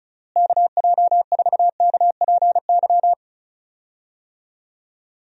Morse code, heard as a keyed tone, about 34 wpm, 700 Hz. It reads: KJ4KPY
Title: USA Call Signs - 6 Characters - Repeated 2x - 35wpm